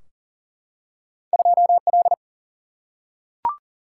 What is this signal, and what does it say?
Morse code, heard as a keyed tone, about 40 wpm, 700 Hz. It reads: 2L
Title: Single-Letter Single-Number 40wpm